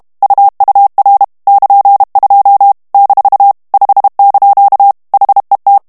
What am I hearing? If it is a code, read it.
Morse code, heard as a keyed tone, about 32 wpm, 800 Hz. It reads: UUR <KN>2 - 5) HET
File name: HOLLANDAIS texte clair 2 vitesse 30 mots minute